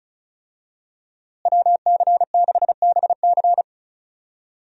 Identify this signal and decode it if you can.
Morse code, heard as a keyed tone, about 36 wpm, 700 Hz. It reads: WC6BC
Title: USA Call Signs - 5 Characters - Repeated 3x - 35wpm